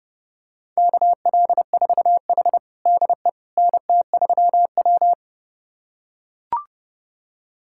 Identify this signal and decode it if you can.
Morse code, heard as a keyed tone, about 30 wpm, 700 Hz. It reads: KL4H DE NT3W
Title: USA Callsign DE Callsign - 4 Characters - 30wpm